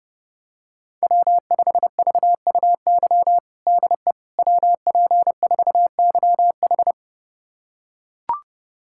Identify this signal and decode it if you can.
Morse code, heard as a keyed tone, about 30 wpm, 700 Hz. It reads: W5VUY DE WP4YH
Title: USA Callsign DE Callsign - 5 Characters - 30wpm